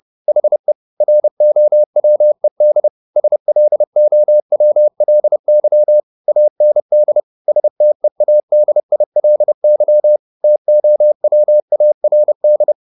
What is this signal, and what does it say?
Morse code, heard as a keyed tone, about 30 wpm, 600 Hz. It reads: HE ROWED SLOWLY AND STEADILY TOWARD
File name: the old man and the sea / the old man and the sea 1 30wpm